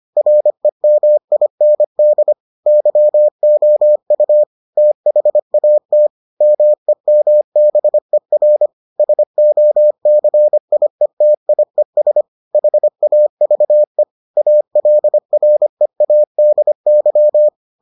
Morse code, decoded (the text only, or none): REMIND YOU THAT MEMBER SOCIETIES HAVE ALREADY